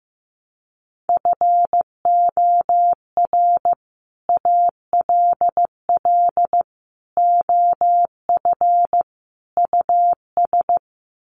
Morse code, heard as a keyed tone, about 15 wpm, 700 Hz. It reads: FOR ALL OF US
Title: Sets of 4 Words 15wpm